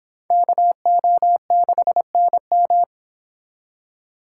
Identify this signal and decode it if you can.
Morse code, heard as a keyed tone, about 30 wpm, 700 Hz. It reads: KO6NM